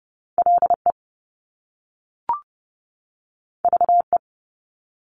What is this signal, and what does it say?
Morse code, heard as a keyed tone, about 30 wpm, 700 Hz. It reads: LE VE
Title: Top Two Letter Combinations in Words 30wpm